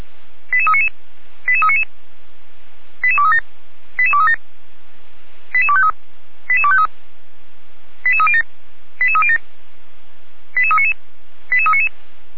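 Five-tone selective call (ZVEI), heard as quick runs of five tones, tones of about 70 ms. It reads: 88399 88399 88237 88237 88253 88253 88307 88307 88399 88399
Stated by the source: ZVEI five-tone selective call received by SDR.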